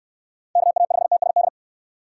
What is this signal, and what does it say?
Morse code, heard as a keyed tone, about 69 wpm, 700 Hz. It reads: KA5TSD